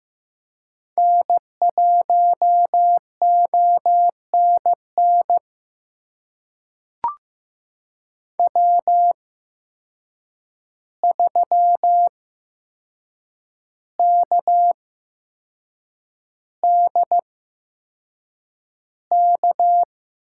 Morse code, heard as a keyed tone, about 15 wpm, spaced as for 6 wpm, 700 Hz. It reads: N1ONN W3KDK
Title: USA Call Signs - 5 Characters - 8x Character Spacing - 15wpm